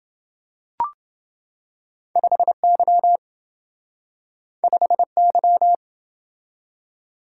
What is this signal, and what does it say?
Morse code, heard as a keyed tone, about 27 wpm, 700 Hz. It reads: 5Y 5Y